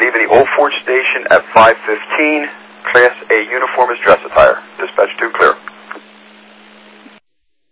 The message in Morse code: none

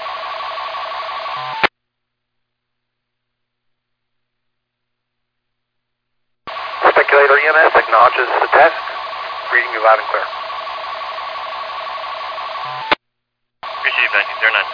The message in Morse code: none

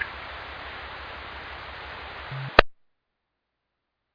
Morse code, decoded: none

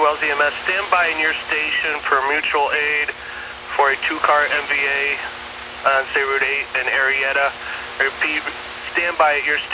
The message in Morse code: none

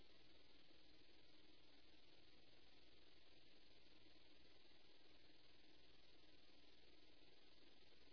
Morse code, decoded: none

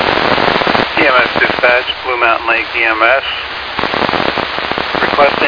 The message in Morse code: none

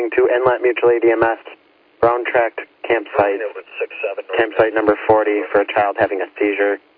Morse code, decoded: none

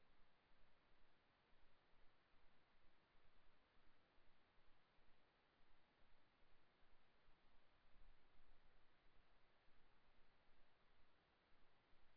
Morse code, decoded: none

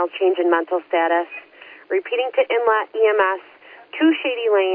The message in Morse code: none

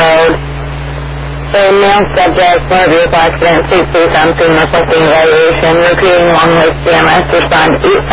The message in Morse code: none

